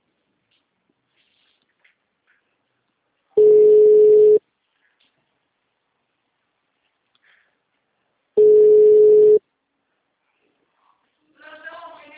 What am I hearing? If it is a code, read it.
Ringback tone (Europe, 425 Hz).